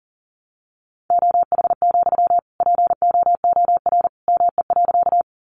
Morse code, heard as a keyed tone, about 40 wpm, 700 Hz. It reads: OH, POOR ME.
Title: Sentences from Top 1000 Words 40wpm